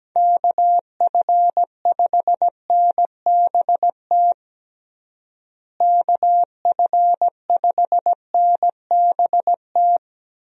Morse code, heard as a keyed tone, about 17 wpm, 700 Hz. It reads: KF5NBT KF5NBT
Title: USA Call Signs - 6 Characters - Repeated 3x - 17wpm